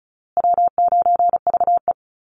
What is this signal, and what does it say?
Morse code, heard as a keyed tone, about 34 wpm, 700 Hz. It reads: W9VE
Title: Call Signs Repeated 2x - 35wpm